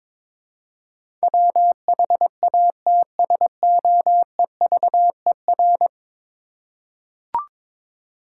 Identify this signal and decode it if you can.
Morse code, heard as a keyed tone, about 22 wpm, 700 Hz. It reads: WHATSOEVER
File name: practice / 22wpm / 10 Letter Words - First 3 Letters Given - 22wpm